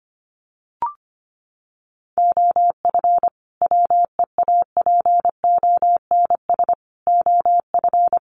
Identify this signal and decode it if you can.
Morse code, heard as a keyed tone, about 25 wpm, 700 Hz. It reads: OF WEAPONS OF
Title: Sets of 5 Words Encore 25wpm